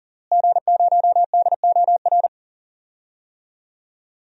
Morse code, heard as a keyed tone, about 37 wpm, 700 Hz. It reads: G0DOR
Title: DXCC-100 Call Signs - 5 Characters - Repeated 2x - 40wpm